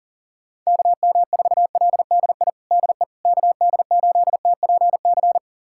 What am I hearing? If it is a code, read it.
Morse code, heard as a keyed tone, about 37 wpm, 700 Hz. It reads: KM4LDI DE KD8TPC
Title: USA Callsign DE Callsign - 6 Characters - 40wpm